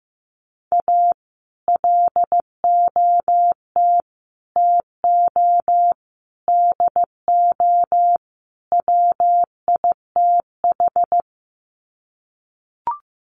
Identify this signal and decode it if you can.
Morse code, heard as a keyed tone, about 15 wpm, 700 Hz. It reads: A LOT TO DO WITH
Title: Sets of 5 Words 15wpm